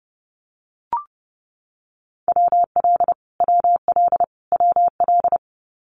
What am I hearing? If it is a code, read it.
Morse code, heard as a keyed tone, about 30 wpm, 700 Hz. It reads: WL WL WL
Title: Abbreviations Repeated 3x - 30wpm